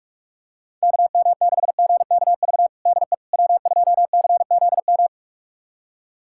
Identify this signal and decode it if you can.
Morse code, heard as a keyed tone, about 45 wpm, 700 Hz. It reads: KM6GKV DE W2CZM
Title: USA Callsign DE Callsign 45wpm